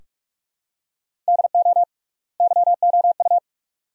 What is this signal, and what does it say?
Morse code, heard as a keyed tone, about 46 wpm, 700 Hz. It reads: DO YOU
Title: 3x Word-Spacing - Sentences from Top 400 Words - 45wpm